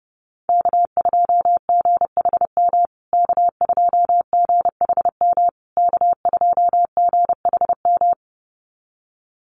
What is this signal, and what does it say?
Morse code, heard as a keyed tone, about 28 wpm, 700 Hz. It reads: K2GHM K2GHM K2GHM
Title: USA Call Signs Repeated 3x 30wpm